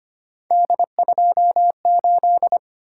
Morse code, heard as a keyed tone, about 25 wpm, 700 Hz. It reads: D28
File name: Three Letter-Number Combinations - 25wpm